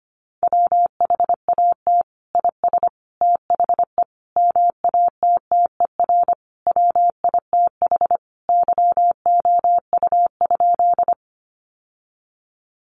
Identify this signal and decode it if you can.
Morse code, heard as a keyed tone, about 25 wpm, 700 Hz. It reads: WHAT IS THE MATTER WITH YOU?
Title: Sentences from Top 700 Words 25wpm